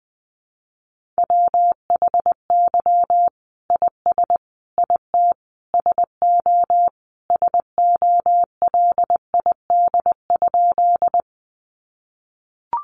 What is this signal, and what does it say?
Morse code, heard as a keyed tone, about 19 wpm, 700 Hz. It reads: WHY IS IT SO SOLID?
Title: Sentences from Top 1500 Words 20wpm